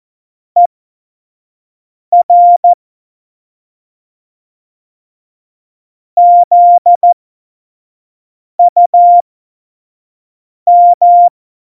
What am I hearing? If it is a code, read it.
Morse code, heard as a keyed tone, about 14 wpm, spaced as for 5 wpm, 700 Hz. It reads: ER ZUM
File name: Morsekurs / Witze / Witz 14. August 2018, 25 F70